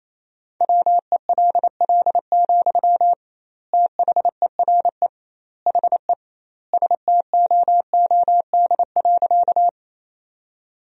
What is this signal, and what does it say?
Morse code, heard as a keyed tone, about 28 wpm, 700 Hz. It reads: WELL, THERE HE STOOD.
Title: 2x Word-Spacing - Sentences from Top 500 Words - 28wpm